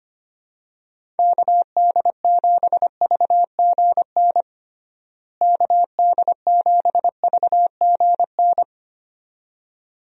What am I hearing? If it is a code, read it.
Morse code, heard as a keyed tone, about 25 wpm, 700 Hz. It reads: KD7VGN KD7VGN
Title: USA Call Signs - 6 Characters - Repeated 2x - 25wpm